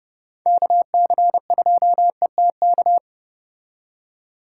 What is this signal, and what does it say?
Morse code, heard as a keyed tone, about 35 wpm, 700 Hz. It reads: KC2ETK